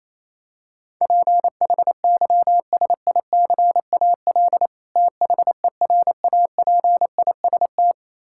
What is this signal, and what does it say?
Morse code, heard as a keyed tone, about 28 wpm, 700 Hz. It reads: PHYSICAL THERAPIST
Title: Practice QSO - Exchange 3 - More Info - 28wpm